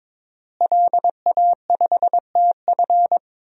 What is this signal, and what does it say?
Morse code, heard as a keyed tone, about 22 wpm, 700 Hz. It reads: LA5TF
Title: DXCC-100 Call Signs - 5 Characters - Repeated 2x - 22wpm